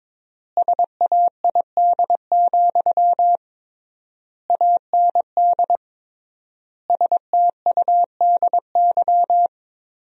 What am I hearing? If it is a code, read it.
Morse code, heard as a keyed tone, about 22 wpm, 700 Hz. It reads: SAID, AND STUDY
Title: 3x Word-Spacing - Sentences from Top 300 Words - 22wpm